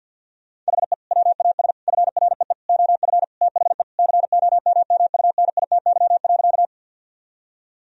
Morse code, heard as a keyed tone, about 50 wpm, 700 Hz. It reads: HE WAS FREE OF THE COMMUNITY.
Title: Sentences from Top 1000 Words - Review Part 4 - 50wpm